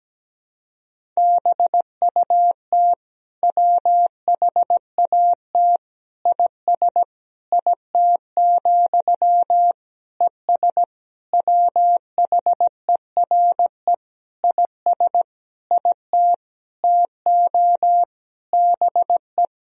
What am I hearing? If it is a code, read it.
Morse code, heard as a keyed tone, about 17 wpm, 700 Hz. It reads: BUT WHAT IS IT, ES WHERE IS IT TO BE
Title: Sentences from Top 300 Words and Abbreviations 17wpm